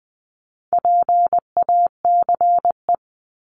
Morse code, heard as a keyed tone, about 20 wpm, 700 Hz. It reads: PACE
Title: Top 1751-2000 Words 20wpm